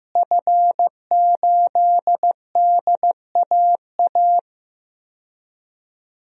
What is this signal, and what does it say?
Morse code, heard as a keyed tone, about 15 wpm, 700 Hz. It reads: F8DAA
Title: DXCC-100 Call Signs - 5 Characters - Repeated 3x - 15wpm